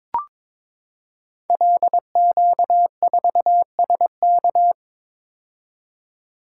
Morse code, heard as a keyed tone, about 22 wpm, 700 Hz. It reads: LQ4SK
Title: Code Groups - 5 Letter:Numbers - 22wpm